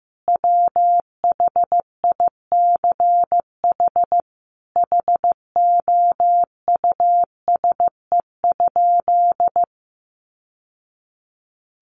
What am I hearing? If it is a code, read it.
Morse code, heard as a keyed tone, about 15 wpm, 700 Hz. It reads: WHICH HOUSE?
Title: Sentences from Top 1100 Words 15wpm